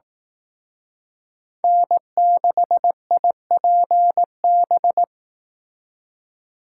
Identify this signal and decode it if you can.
Morse code, heard as a keyed tone, about 18 wpm, 700 Hz. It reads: N6IPB
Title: Speed Racing - USA 1x3 Call Signs 15wpm